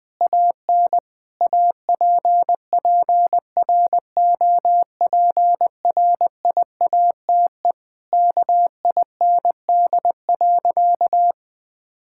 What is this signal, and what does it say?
Morse code, heard as a keyed tone, about 21 wpm, 700 Hz. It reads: AN APPROPRIATE KIND.
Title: Sentences from Top 1000 Words - Review Part 5 - 20wpm